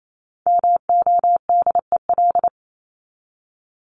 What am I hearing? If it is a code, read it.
Morse code, heard as a keyed tone, about 28 wpm, 700 Hz. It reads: MODEL
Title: Top 501-750 Words 28wpm